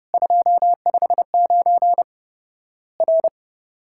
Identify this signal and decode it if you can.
Morse code, heard as a keyed tone, about 30 wpm, 700 Hz. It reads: 259 R
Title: Worked All Europe DX Contest - QTC Exchange - 30wpm